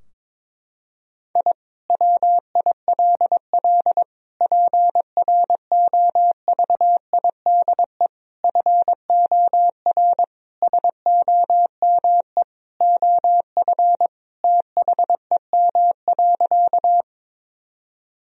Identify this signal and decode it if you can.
Morse code, heard as a keyed tone, about 22 wpm, 700 Hz. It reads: I WILL PROVIDE FOR SOME OF THEM.
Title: Sentences from Top 1000 Words - Review Part 3 - 22wpm